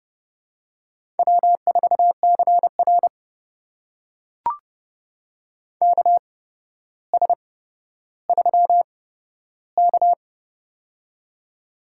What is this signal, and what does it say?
Morse code, heard as a keyed tone, about 30 wpm, spaced as for 13 wpm, 700 Hz. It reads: W4CR KS3K
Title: USA Call Signs - 4 Characters - 8x Character Spacing - 30wpm